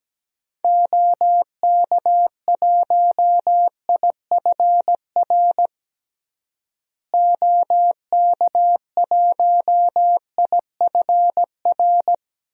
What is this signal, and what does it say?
Morse code, heard as a keyed tone, about 17 wpm, 700 Hz. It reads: OK1IFR OK1IFR
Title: DXCC-100 Call Signs - 6 Characters - Repeated 3x - 17wpm